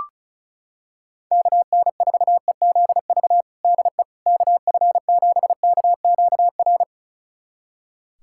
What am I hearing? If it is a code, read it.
Morse code, heard as a keyed tone, about 36 wpm, 690 Hz. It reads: KN4EZV DE KF7KQR
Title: USA Callsign DE Callsign 35wpm